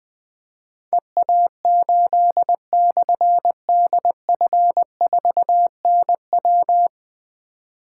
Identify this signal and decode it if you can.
Morse code, heard as a keyed tone, about 20 wpm, 700 Hz. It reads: EA8/DF4NW
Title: Difficult Call Signs 20wpm